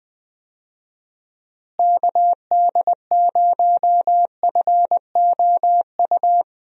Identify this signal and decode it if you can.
Morse code, heard as a keyed tone, about 21 wpm, 700 Hz. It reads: KD0FOU
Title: USA Call Signs - 6 Characters - Repeated 2x - 20wpm